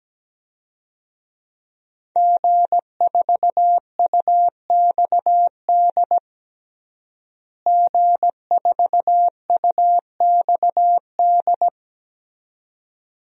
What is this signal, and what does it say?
Morse code, heard as a keyed tone, about 17 wpm, 700 Hz. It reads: G4UXD G4UXD
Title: DXCC-100 Call Signs - 5 Characters - Repeated 2x - 17wpm